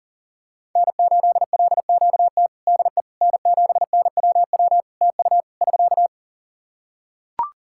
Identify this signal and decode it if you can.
Morse code, heard as a keyed tone, about 40 wpm, 700 Hz. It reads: N8LQT DE N7NWW TU <SK>
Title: Practice QSO - Exchange 4 - Ending - 40wpm